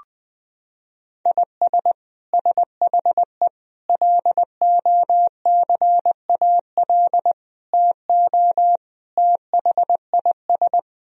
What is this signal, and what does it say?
Morse code, heard as a keyed tone, about 20 wpm, 700 Hz. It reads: IS SHE LOCAL TO THIS
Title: Sentences from Top 1000 Words - Review Part 3 - 20wpm